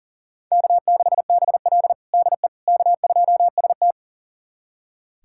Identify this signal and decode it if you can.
Morse code, heard as a keyed tone, about 40 wpm, 700 Hz. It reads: K6BL DE K2ST
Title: USA Callsign DE Callsign - 4 Characters - 40wpm